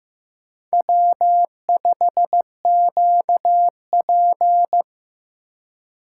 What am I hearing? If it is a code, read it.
Morse code, heard as a keyed tone, about 15 wpm, 700 Hz. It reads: W5QP
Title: K1USN SST QSO Practice - 15wpm